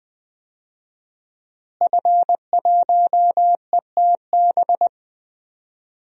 Morse code, heard as a keyed tone, about 20 wpm, 700 Hz. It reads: F1ETB
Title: DXCC-100 Call Signs - 5 Characters - Repeated 3x - 20wpm